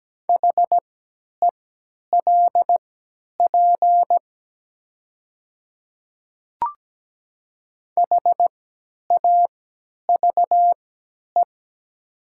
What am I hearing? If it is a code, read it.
Morse code, heard as a keyed tone, about 17 wpm, spaced as for 10 wpm, 700 Hz. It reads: HELP HAVE
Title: 3x Character Spacing - Top 200 Words - 17wpm